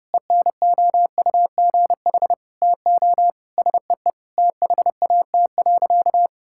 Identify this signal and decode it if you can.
Morse code, heard as a keyed tone, about 30 wpm, 700 Hz. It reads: ENOUGH TO SEE THAT.
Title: Sentences from Top 1600 Words 30wpm